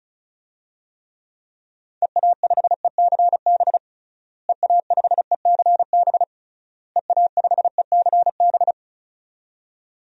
Morse code, heard as a keyed tone, about 36 wpm, 700 Hz. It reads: EA5ECB EA5ECB EA5ECB
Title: DXCC-100 Call Signs - 6 Characters - Repeated 3x - 35wpm